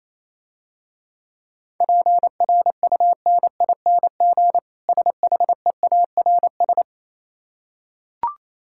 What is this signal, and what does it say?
Morse code, heard as a keyed tone, about 27 wpm, 690 Hz. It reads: PRUNING SHEARS
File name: Easy Words - Speed-Racing with Hints - 22wpm